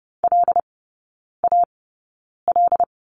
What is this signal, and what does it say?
Morse code, heard as a keyed tone, about 30 wpm, spaced as for 9 wpm, 700 Hz. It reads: LAL
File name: Lesson 16e - Letter c - Three Characters 30wpm